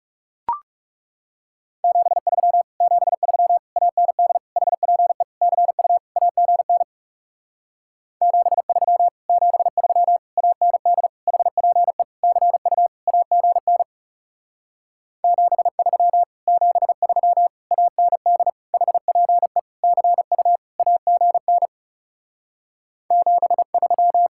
Morse code, heard as a keyed tone, about 37 wpm, 700 Hz. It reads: 73 73 AND HPE CU AGN 73 73 AND HPE CU AGN 73 73 AND HPE CU AGN 73
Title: Speed-Racing Common QSO Phrases - 30wpm